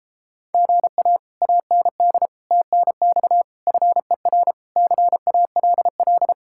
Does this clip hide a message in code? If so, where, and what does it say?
Yes: Morse code from 0.5 to 6.3 s, heard as a keyed tone, about 31 wpm, 700 Hz. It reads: GA AND TNX FER CALL